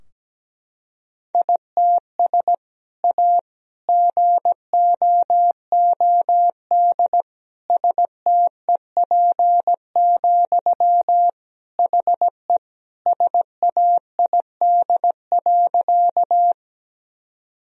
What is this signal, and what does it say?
Morse code, heard as a keyed tone, about 17 wpm, 700 Hz. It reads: ITS A GOOD STEP, HE SAID.
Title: Sentences from Top 1000 Words - Review Part 2 - 17wpm